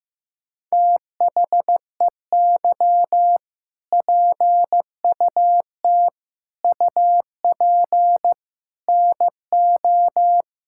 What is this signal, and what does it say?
Morse code, heard as a keyed tone, about 16 wpm, 700 Hz. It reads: THEY PUT UP NO
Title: Sentences from Top 1600 Words 15wpm